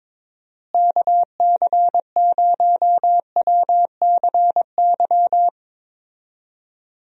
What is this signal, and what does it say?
Morse code, heard as a keyed tone, about 22 wpm, 700 Hz. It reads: KC0WCY
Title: USA Call Signs - 6 Characters - Repeated 3x - 22wpm